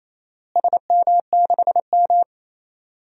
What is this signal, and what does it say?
Morse code, heard as a keyed tone, about 27 wpm, 700 Hz. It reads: SM6M
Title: DXCC-100 Call Signs - 4 Characters - Repeated 2x - 28wpm